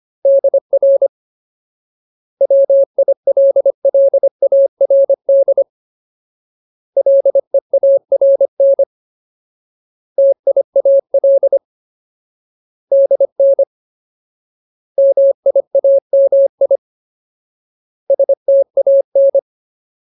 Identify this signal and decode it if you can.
Morse code, heard as a keyed tone, about 25 wpm, 550 Hz. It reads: DR WILLARD LEARN TIAL DN MIAMI STAN
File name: Ultimate Word List / 25WPM Ultimate